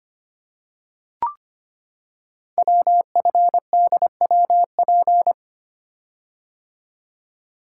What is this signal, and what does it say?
Morse code, heard as a keyed tone, about 25 wpm, 700 Hz. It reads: WFDWP
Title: Code Groups - 5 Letters - 25wpm